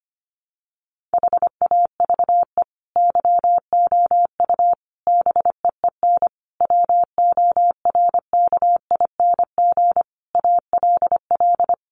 Morse code, heard as a keyed tone, about 25 wpm, 700 Hz. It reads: HAVE YOU BEEN WORKING ALL
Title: Sentences from Top 700 Words 25wpm